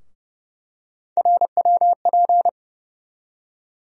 Morse code, heard as a keyed tone, about 30 wpm, 700 Hz. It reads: RWP
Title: USA Call Sign Postfixes - 3 Letters 30wpm